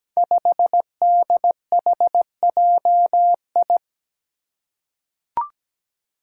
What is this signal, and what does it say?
Morse code, heard as a keyed tone, about 17 wpm, 700 Hz. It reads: 5DHJI
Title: Five Challenging Letter-Number Combinations 17wpm